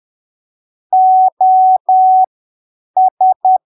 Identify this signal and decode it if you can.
Morse code, heard as a keyed tone, about 10 wpm, spaced as for 7 wpm, 750 Hz. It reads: OS